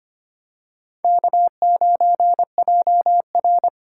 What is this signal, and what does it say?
Morse code, heard as a keyed tone, about 25 wpm, 700 Hz. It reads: K9JR
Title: USA Call Signs - 2x Character Spacing - 25wpm